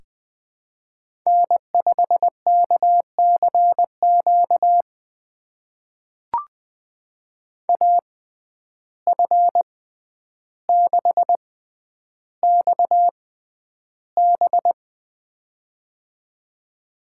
Morse code, heard as a keyed tone, about 20 wpm, spaced as for 10 wpm, 700 Hz. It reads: N5KCQ AF6XB
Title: USA Call Signs - 5 Characters - 6x Character Spacing - 20wpm